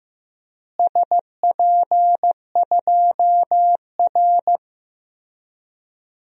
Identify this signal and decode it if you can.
Morse code, heard as a keyed tone, about 15 wpm, 700 Hz. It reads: SP2R
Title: DXCC-100 Call Signs - 4 Characters - Repeated 3x - 15wpm